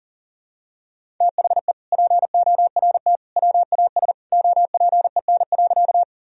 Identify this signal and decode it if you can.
Morse code, heard as a keyed tone, about 40 wpm, 700 Hz. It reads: THE PORT WAS OPEN.
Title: Sentences from Top 1000 Words - Review Part 1 - 40wpm